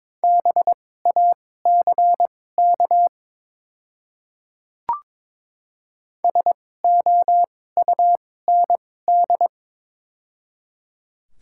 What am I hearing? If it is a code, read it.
Morse code, heard as a keyed tone, about 22 wpm, spaced as for 16 wpm, 700 Hz. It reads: BACK SOUND
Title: 2x Character Spacing - Top 200 Words - 22wpm